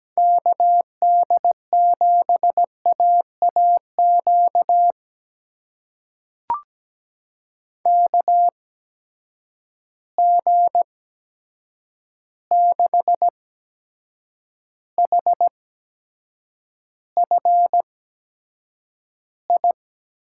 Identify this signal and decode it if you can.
Morse code, heard as a keyed tone, about 17 wpm, spaced as for 7 wpm, 700 Hz. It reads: KD7AAQ KG6HFI
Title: USA Call Signs - 6 Characters - 8x Character Spacing - 17wpm